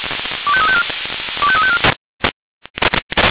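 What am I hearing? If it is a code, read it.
Five-tone selective call (ZVEI), heard as quick runs of five tones, tones of about 70 ms. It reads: 25354 25354
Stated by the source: ZVEI five-tone selective call received on an SDR.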